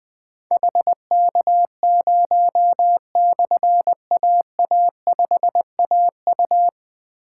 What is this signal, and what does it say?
Morse code, heard as a keyed tone, about 20 wpm, 700 Hz. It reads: HK0/AA5AU
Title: Difficult Call Signs 20wpm